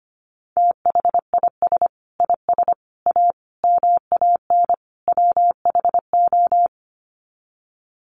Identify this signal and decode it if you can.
Morse code, heard as a keyed tone, about 25 wpm, 700 Hz. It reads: THIS IS A MAN WHO
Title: Sets of 5 Words Encore2 25wpm